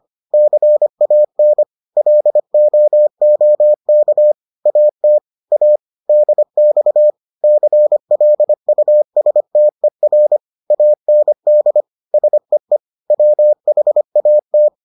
Morse code, heard as a keyed tone, about 25 wpm, 600 Hz. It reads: CAN LOOK AT A DX CLUSTER AND SEE WHAT